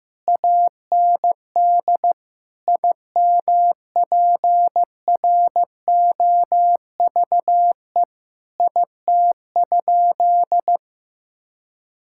Morse code, heard as a keyed tone, about 14 wpm, 700 Hz. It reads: AND IMPROVE IT?
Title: Sentences from Top 1000 Words - Review Part 5 - 15wpm